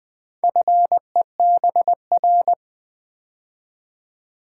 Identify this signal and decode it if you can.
Morse code, heard as a keyed tone, about 20 wpm, 700 Hz. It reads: FEBR
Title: Build A Word - 2 Letters At A Time - 8-Letter Words - 20wpm